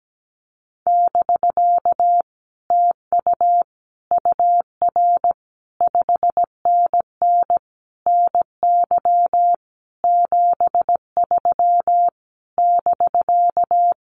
Morse code, heard as a keyed tone, about 17 wpm, 700 Hz. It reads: <BK> TU UR 5NN NY 73 <BK>
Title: Common QSO Phrases Encore 17wpm